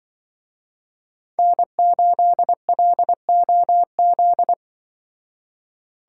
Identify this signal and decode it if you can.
Morse code, heard as a keyed tone, about 24 wpm, 700 Hz. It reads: N8LOZ
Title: Speed Racing - USA 1x3 Call Signs 20wpm